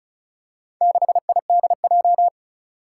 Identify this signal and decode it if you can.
Morse code, heard as a keyed tone, about 34 wpm, 700 Hz. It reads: 6IDJ
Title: Four Challenging Letter-Number Combinations 35wpm